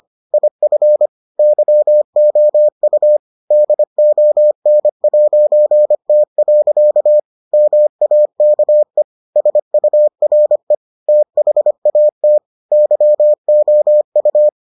Morse code, heard as a keyed tone, about 25 wpm, 600 Hz. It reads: IF YOU DON'T. MAKE SURE THAT YOU